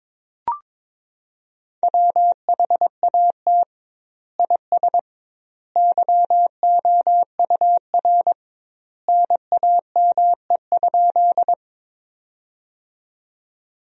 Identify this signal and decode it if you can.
Morse code, heard as a keyed tone, about 22 wpm, 700 Hz. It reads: WHAT IS YOUR NAME?
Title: 2x Word-Spacing - Sentences from Top 200 Words - 22wpm